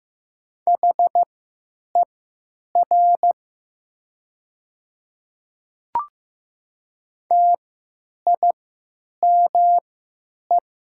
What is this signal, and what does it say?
Morse code, heard as a keyed tone, about 15 wpm, spaced as for 9 wpm, 700 Hz. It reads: HER TIME
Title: 3x Character Spacing - Top 100 Words - 15wpm